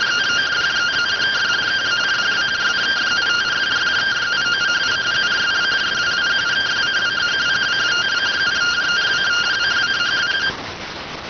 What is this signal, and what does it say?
Two-tone teleprinter signal, about 170 Hz shift.